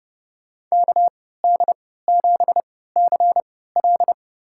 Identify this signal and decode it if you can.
Morse code, heard as a keyed tone, about 30 wpm, spaced as for 17 wpm, 700 Hz. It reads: KD7CL
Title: USA Call Signs - 5 Characters - 3x Character Spacing - 30wpm